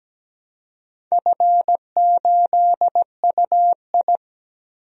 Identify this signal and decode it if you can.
Morse code, heard as a keyed tone, about 17 wpm, 700 Hz. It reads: F8UI
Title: DXCC-100 Call Signs - 4 Characters - Repeated 2x - 17wpm